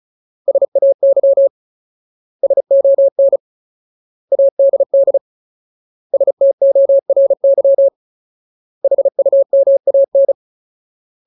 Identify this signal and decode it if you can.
Morse code, heard as a keyed tone, about 34 wpm, 550 Hz. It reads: SAY SON ADD STORY HUMAN
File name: Ultimate Word List / 35WPM Ultimate